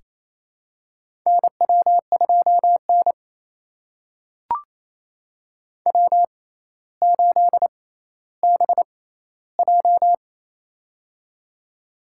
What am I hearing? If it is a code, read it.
Morse code, heard as a keyed tone, about 28 wpm, spaced as for 14 wpm, 700 Hz. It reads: NW2N W8BJ
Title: USA Call Signs - 4 Characters - 6x Character Spacing - 28wpm